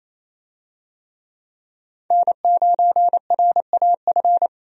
Morse code, heard as a keyed tone, about 28 wpm, 700 Hz. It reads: N9RAF
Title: USA Call Signs - 5 Characters - Repeated 2x - 28wpm